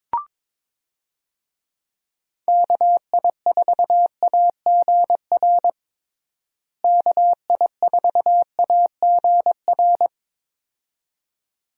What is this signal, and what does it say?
Morse code, heard as a keyed tone, about 22 wpm, 700 Hz. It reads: KI4AGR KI4AGR
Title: USA Call Signs - 6 Characters - Repeated 2x - 22wpm